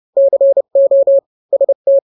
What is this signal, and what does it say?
Morse code, heard as a keyed tone, about 30 wpm, 550 Hz. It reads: CO ST
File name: n-grams / 2-grams 30 25 WPM